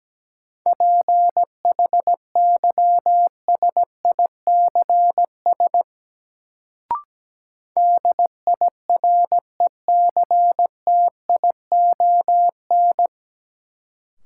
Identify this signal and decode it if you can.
Morse code, heard as a keyed tone, about 17 wpm, 700 Hz. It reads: PHYSICS DIRECTION
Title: Sentences from Top 1000 Words - Review Part 5 - New Word Drill - 17wpm